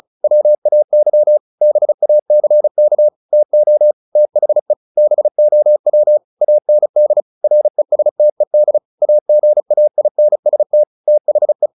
Morse code, heard as a keyed tone, about 34 wpm, 600 Hz. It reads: WAY BACK TO THE BOW AND RESTED AGAINST THE
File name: the old man and the sea / the old man and the sea 1 35wpm